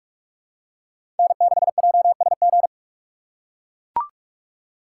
Morse code, heard as a keyed tone, about 44 wpm, 690 Hz. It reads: N6JSG
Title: USA Call Signs - 5 Characters - 2x Character Spacing - 45wpm